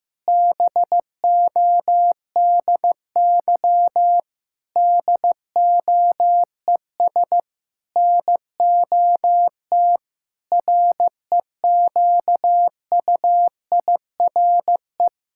Morse code, heard as a keyed tone, about 14 wpm, 700 Hz. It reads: BODY DOES NOT REQUIRE
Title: Sentences from Top 1000 Words - Review Part 6 - 15wpm